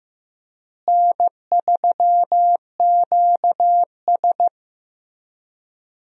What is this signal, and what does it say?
Morse code, heard as a keyed tone, about 15 wpm, 700 Hz. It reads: N3QS
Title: USA Call Signs - 4 Characters - Repeated 3x - 15wpm